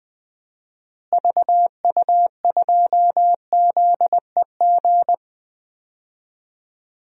Morse code, heard as a keyed tone, about 20 wpm, 700 Hz. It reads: VU2ZEG
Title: DXCC-100 Call Signs - 6 Characters - Repeated 3x - 20wpm